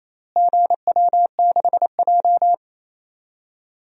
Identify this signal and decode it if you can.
Morse code, heard as a keyed tone, about 28 wpm, 700 Hz. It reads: GW6J
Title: DXCC-100 Call Signs - 4 Characters - Repeated 3x - 28wpm